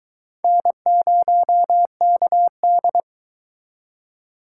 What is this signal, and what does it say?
Morse code, heard as a keyed tone, about 24 wpm, 700 Hz. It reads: N0KD